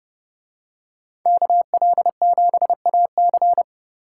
Morse code, heard as a keyed tone, about 30 wpm, 700 Hz. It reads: KL7AC